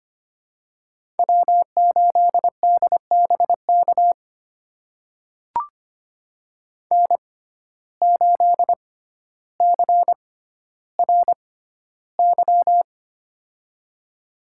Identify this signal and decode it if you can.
Morse code, heard as a keyed tone, about 25 wpm, spaced as for 13 wpm, 700 Hz. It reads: W8DBK N8CRY
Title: USA Call Signs - 6x Character Spacing - 25wpm